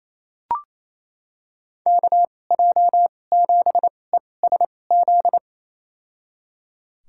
Morse code, heard as a keyed tone, about 28 wpm, spaced as for 20 wpm, 700 Hz. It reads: KJ7ESZ
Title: USA Call Signs - 2x Character Spacing - 28wpm